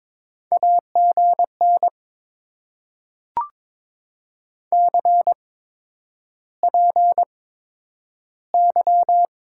Morse code, heard as a keyed tone, about 22 wpm, spaced as for 9 wpm, 700 Hz. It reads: AGN CPY
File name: Top 90 QSO Elements - 8x Character Spacing - 22wpm